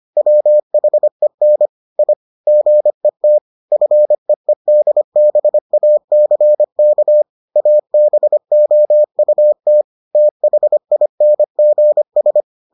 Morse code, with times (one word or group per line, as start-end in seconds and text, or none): 0.2-1.7 WHEN
2.0-2.1 I
2.5-3.4 GET
3.7-7.2 FEEDBACK
7.6-9.8 ABOUT
10.2-12.4 THINGS